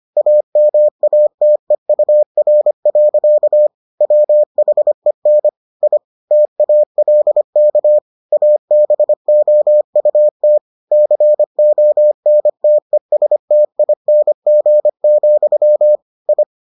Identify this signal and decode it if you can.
Morse code, heard as a keyed tone, about 25 wpm, 600 Hz. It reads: AMATEUR. WHEN I TALK ABOUT CONTESTING, I